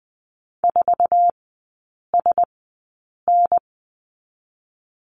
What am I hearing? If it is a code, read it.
Morse code, heard as a keyed tone, about 20 wpm, spaced as for 8 wpm, 690 Hz. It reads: 4SN
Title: 093 Lesson 16e - Letter c - Three Characters 20wpm